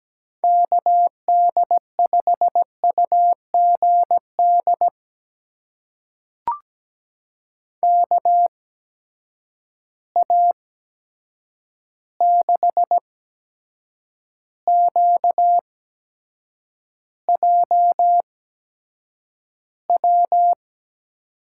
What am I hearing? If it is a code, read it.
Morse code, heard as a keyed tone, about 17 wpm, spaced as for 7 wpm, 700 Hz. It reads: KD5UGD KA6QJW
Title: USA Call Signs - 6 Characters - 8x Character Spacing - 17wpm